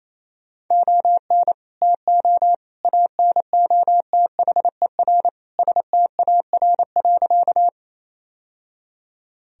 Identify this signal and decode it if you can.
Morse code, heard as a keyed tone, about 28 wpm, 700 Hz. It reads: ON TO ANOTHER STAR.
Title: Sentences from Top 1000 Words - Review Part 2 - 28wpm